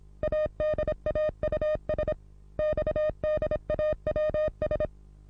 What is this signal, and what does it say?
Morse code, heard as a keyed tone, about 26 wpm, 610 Hz. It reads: ADAUS XDAWS
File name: repos / Morsekurs / DJ4UF Morsen27